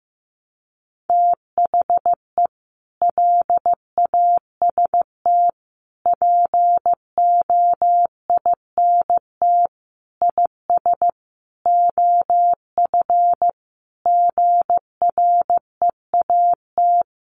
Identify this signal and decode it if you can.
Morse code, heard as a keyed tone, about 15 wpm, 700 Hz. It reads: THE LAST POINT IS OF GREAT
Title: Sentences from Top 1200 Words 15wpm